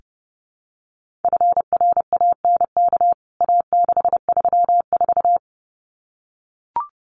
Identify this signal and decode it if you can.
Morse code, heard as a keyed tone, about 30 wpm, 700 Hz. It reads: FRANK A634
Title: CWT Exchange 30wpm